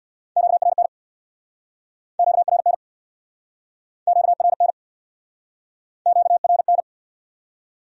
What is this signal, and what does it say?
Morse code, heard as a keyed tone, about 63 wpm, 700 Hz. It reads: QRN QRN QRN QRN